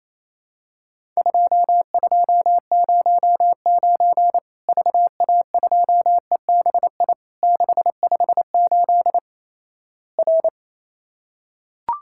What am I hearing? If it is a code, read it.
Morse code, heard as a keyed tone, about 28 wpm, 700 Hz. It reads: 2209 VA2EBI 658 R
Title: Worked All Europe DX Contest - QTC Exchange - 28wpm